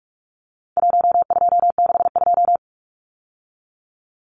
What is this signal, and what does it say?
Morse code, heard as a keyed tone, about 44 wpm, 700 Hz. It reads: 1262
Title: Numbers - 4-Digit 45wpm